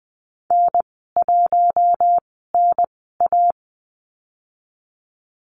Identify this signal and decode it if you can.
Morse code, heard as a keyed tone, about 20 wpm, spaced as for 14 wpm, 700 Hz. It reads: N1NA
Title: 042 Lesson 8g - Number 1 - Call Signs 20wpm